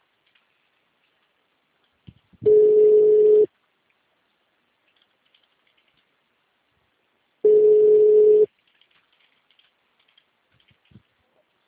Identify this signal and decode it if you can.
Ringback tone (Europe, 425 Hz).